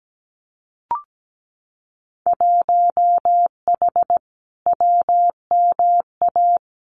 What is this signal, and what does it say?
Morse code, heard as a keyed tone, about 17 wpm, 700 Hz. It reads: 1H WMA
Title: Winter Field Day Exchanges 17wpm